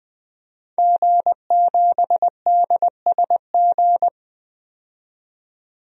Morse code, heard as a keyed tone, about 20 wpm, 700 Hz. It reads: G7DSG
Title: DXCC-100 Call Signs - 5 Characters - 20wpm